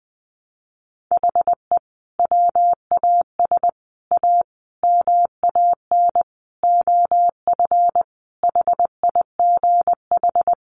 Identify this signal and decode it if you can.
Morse code, heard as a keyed tone, about 20 wpm, 700 Hz. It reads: HE WAS A MAN OF HIGH
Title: Sentences from Top 1200 Words 20wpm